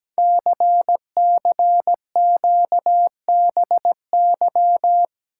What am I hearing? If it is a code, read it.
Morse code, heard as a keyed tone, about 17 wpm, 700 Hz. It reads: CCQBY
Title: Code Groups - 5 Letters - 17wpm